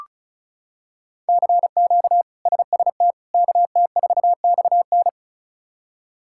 Morse code, heard as a keyed tone, about 36 wpm, 700 Hz. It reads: CQ SST KT4XN
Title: K1USN SST QSO Practice - 35wpm